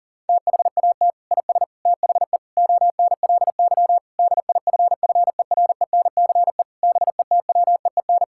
Morse code, heard as a keyed tone, about 40 wpm, 700 Hz. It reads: THAT IS THE ONLY DIFFERENCE BETWEEN
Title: Sentences from Top 1000 Words - Review Part 5 - 40wpm